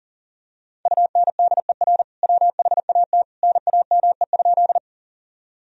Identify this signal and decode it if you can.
Morse code, heard as a keyed tone, about 40 wpm, 700 Hz. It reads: UNDER WHAT NAME?
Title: Sentences from Top 1000 Words - Review Part 1 - 40wpm